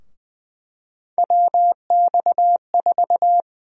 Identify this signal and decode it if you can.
Morse code, heard as a keyed tone, about 20 wpm, 700 Hz. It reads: WX4
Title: USA Call Sign Prefixes - 20wpm